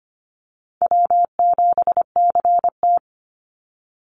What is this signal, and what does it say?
Morse code, heard as a keyed tone, about 25 wpm, 700 Hz. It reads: W7CT
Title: Call Signs - 25wpm